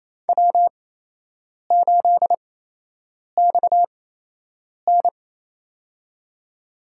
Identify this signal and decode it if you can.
Morse code, heard as a keyed tone, about 28 wpm, spaced as for 8 wpm, 700 Hz. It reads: W8XN
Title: USA Call Signs - 4 Characters - 8x Character Spacing - 28wpm